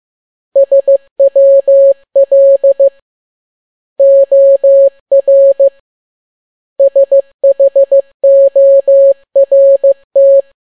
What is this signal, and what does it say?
Morse code, heard as a keyed tone, about 15 wpm, 550 Hz. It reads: SWL OR SHORT
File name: n6tt p1 15C13